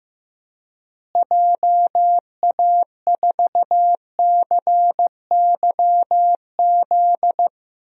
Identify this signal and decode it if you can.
Morse code, heard as a keyed tone, about 16 wpm, 700 Hz. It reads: JA4CYZ
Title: DXCC-100 Call Signs - 6 Characters - Repeated 3x - 15wpm